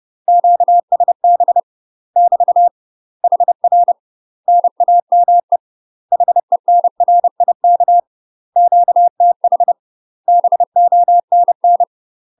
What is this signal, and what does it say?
Morse code, heard as a keyed tone, about 30 wpm, 700 Hz. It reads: QSB <BT> HR NAME HENRIK QTH BONN
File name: UK MASTER QSOs TRAINING/30WPM